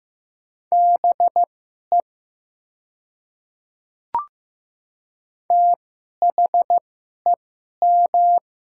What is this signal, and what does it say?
Morse code, heard as a keyed tone, about 15 wpm, spaced as for 11 wpm, 700 Hz. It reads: BE THEM
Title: 2x Character Spacing - Top 200 Words - 15wpm